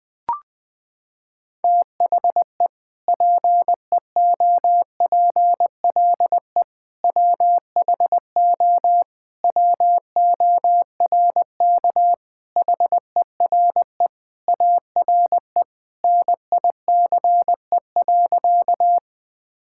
Morse code, heard as a keyed tone, about 20 wpm, 700 Hz. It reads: THE PEOPLE WHO WORK HERE ARE NICE.
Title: Sentences with 3-Word Phrases - 20wpm